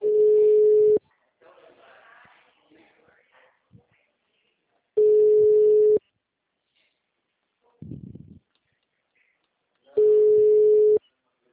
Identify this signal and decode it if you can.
Ringback tone (Europe, 425 Hz), with speech.